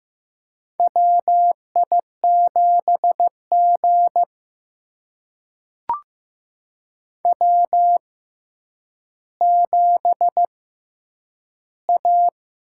Morse code, heard as a keyed tone, about 15 wpm, spaced as for 9 wpm, 700 Hz. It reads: WI7G W7A
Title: USA Call Signs - 4 Characters - 6x Character Spacing - 15wpm